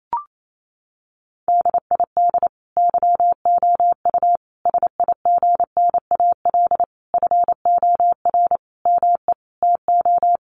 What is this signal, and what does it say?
Morse code, heard as a keyed tone, about 28 wpm, 700 Hz. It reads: DID YOU SIGNAL FOR ME TO
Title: Sentences from Top 1400 Words 28wpm